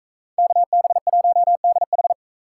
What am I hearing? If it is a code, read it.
Morse code, heard as a keyed tone, about 40 wpm, 700 Hz. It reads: KB1DH